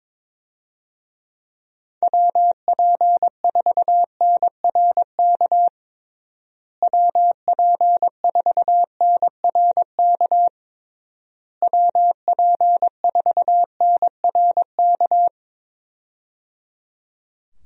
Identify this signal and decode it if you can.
Morse code, heard as a keyed tone, about 22 wpm, 700 Hz. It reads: WP4NRK WP4NRK WP4NRK
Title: USA Call Signs - 6 Characters - Repeated 3x - 22wpm